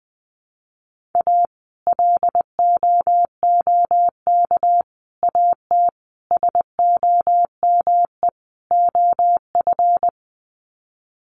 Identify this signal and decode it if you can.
Morse code, heard as a keyed tone, about 20 wpm, 700 Hz. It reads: A LOOK AT SOME OF
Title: Sets of 5 Words Encore2 20wpm